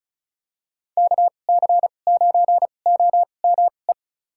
Morse code, heard as a keyed tone, about 34 wpm, spaced as for 25 wpm, 700 Hz. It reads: KC9OME
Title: USA Call Signs - 2x Character Spacing - 35wpm